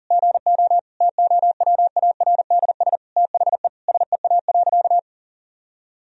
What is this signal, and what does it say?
Morse code, heard as a keyed tone, about 40 wpm, 700 Hz. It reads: GO TOWARDS THE SEA.
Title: Sentences from Top 1000 Words - Review Part 2 - 40wpm